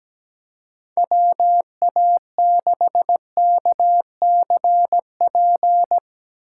Morse code, heard as a keyed tone, about 17 wpm, 700 Hz. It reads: WA6KCP
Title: USA Call Signs - 6 Characters - 3x Character Spacing - 17wpm